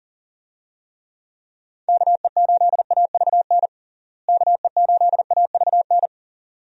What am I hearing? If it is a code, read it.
Morse code, heard as a keyed tone, about 40 wpm, 700 Hz. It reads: KE8AVN KE8AVN
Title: USA Call Signs - 6 Characters - Repeated 3x - 40wpm